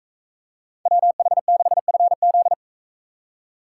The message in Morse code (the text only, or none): WH6FZ